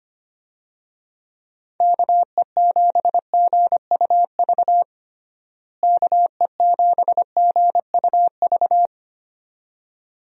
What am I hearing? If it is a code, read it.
Morse code, heard as a keyed tone, about 25 wpm, 700 Hz. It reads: KE7GUV KE7GUV
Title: USA Call Signs - 6 Characters - Repeated 2x - 25wpm